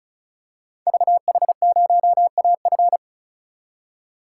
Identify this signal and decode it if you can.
Morse code, heard as a keyed tone, about 36 wpm, 700 Hz. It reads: VH0AF
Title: Code Groups - 5 Letter:Numbers - 35wpm